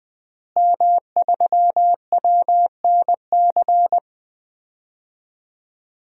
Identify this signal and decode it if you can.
Morse code, heard as a keyed tone, about 20 wpm, 700 Hz. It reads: M3WNC
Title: DXCC-100 Call Signs - 5 Characters - Repeated 2x - 20wpm